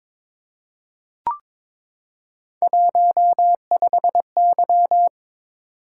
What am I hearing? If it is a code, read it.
Morse code, heard as a keyed tone, about 22 wpm, 700 Hz. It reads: 15Y
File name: practice / 22wpm / Three Letter-Number Combinations - 22wpm